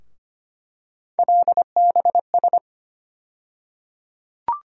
Morse code, heard as a keyed tone, about 25 wpm, 700 Hz. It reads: LBS
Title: USA Call Sign Postfixes - 3 Letters 25wpm